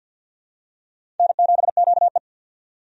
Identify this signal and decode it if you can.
Morse code, heard as a keyed tone, about 48 wpm, 700 Hz. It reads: N7QE
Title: USA Call Signs - 4 Characters - 50wpm